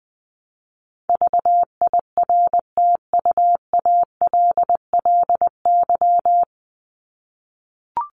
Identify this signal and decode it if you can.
Morse code, heard as a keyed tone, about 20 wpm, 700 Hz. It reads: VIRTUALLY
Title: LY Suffix - Top 100 Words with Base Words 20wpm